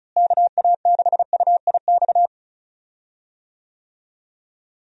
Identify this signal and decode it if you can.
Morse code, heard as a keyed tone, about 34 wpm, 700 Hz. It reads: KA6UIX
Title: USA Call Signs - 6 Characters - Repeated 2x - 35wpm